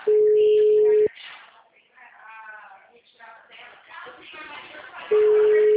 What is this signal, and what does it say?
Ringback tone (Europe, 425 Hz), with speech.
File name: out- -2003- - - .48488